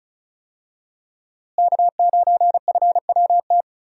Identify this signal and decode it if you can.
Morse code, heard as a keyed tone, about 36 wpm, 700 Hz. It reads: K9FWT
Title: USA Call Signs - 5 Characters - Repeated 3x - 35wpm